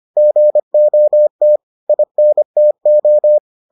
Morse code, heard as a keyed tone, about 25 wpm, 600 Hz. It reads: GOT INTO